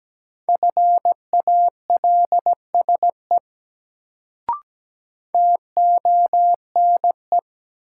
Morse code, heard as a keyed tone, about 17 wpm, 700 Hz. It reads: FALSE TONE
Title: Sentences from Top 1600 Words - New Word Drill - 17wpm